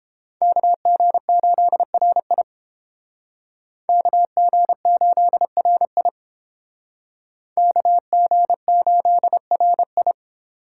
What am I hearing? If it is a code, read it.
Morse code, heard as a keyed tone, about 31 wpm, 700 Hz. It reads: KG8RI KG8RI KG8RI